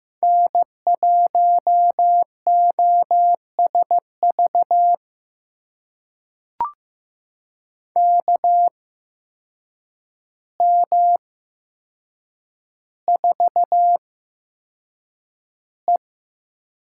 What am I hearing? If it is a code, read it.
Morse code, heard as a keyed tone, about 15 wpm, spaced as for 7 wpm, 700 Hz. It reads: N1OSV KM4E
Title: USA Call Signs - 5 Characters - 8x Character Spacing - 15wpm